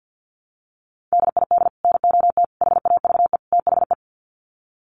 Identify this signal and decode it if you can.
Morse code, heard as a keyed tone, about 50 wpm, 700 Hz. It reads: DID NOT HAVE THE
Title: Sets of 4 Words 50wpm